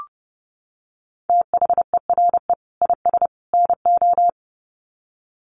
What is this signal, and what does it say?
Morse code, heard as a keyed tone, about 30 wpm, 700 Hz. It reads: THERE IS NO
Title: Sets of 3 Words 30wpm